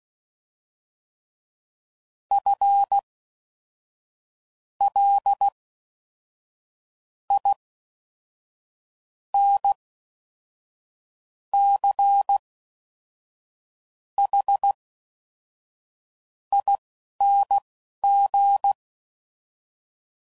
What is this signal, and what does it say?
Morse code, heard as a keyed tone, about 16 wpm, spaced as for 5 wpm, 800 Hz. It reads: FLINCHING